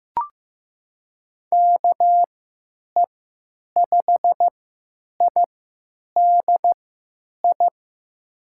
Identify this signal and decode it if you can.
Morse code, heard as a keyed tone, about 15 wpm, spaced as for 9 wpm, 700 Hz. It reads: KE5IDI
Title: USA Call Signs - 6 Characters - 3x Character Spacing - 15wpm